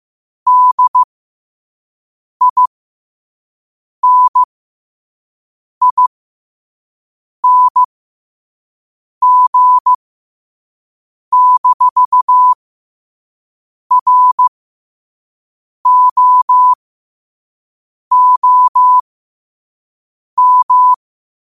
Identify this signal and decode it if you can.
Morse code, heard as a keyed tone, about 15 wpm, spaced as for 5 wpm, 1000 Hz. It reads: DINING-ROOM
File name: qotdfiles / 5wpm